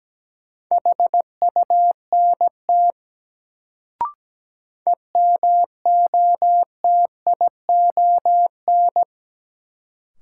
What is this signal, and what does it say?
Morse code, heard as a keyed tone, about 17 wpm, 700 Hz. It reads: HUNT EMOTION
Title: Sentences from Top 1600 Words - New Word Drill Alt - 17wpm